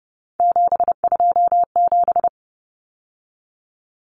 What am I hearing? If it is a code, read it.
Morse code, heard as a keyed tone, about 30 wpm, 700 Hz. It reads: 727
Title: Numbers - 3-Digit 30wpm